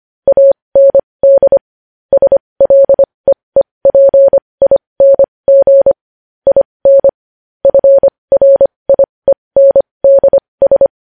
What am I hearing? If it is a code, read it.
Morse code, heard as a keyed tone, about 25 wpm, spaced as for 20 wpm, 550 Hz. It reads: AND SLEEPING IN FRIENDS
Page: news-headlines-in-morse-code-at-20-wpm